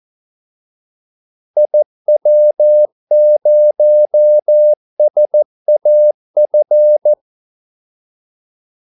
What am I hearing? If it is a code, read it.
Morse code, heard as a keyed tone, about 14 wpm, 600 Hz. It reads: IW0SAF
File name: Callsigns-14wpm